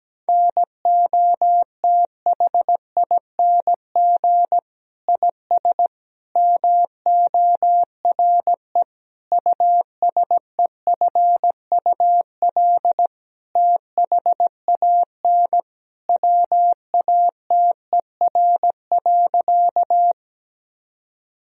Morse code, heard as a keyed tone, about 17 wpm, 700 Hz. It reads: NOTHING IS MORE USEFUL THAN WATER.
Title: Sentences from Top 1000 Words - Review Part 4 - 17wpm